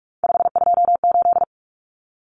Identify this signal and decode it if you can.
Morse code, heard as a keyed tone, about 44 wpm, 700 Hz. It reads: 528
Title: Common Cut Numbers - 3-Digit 45wpm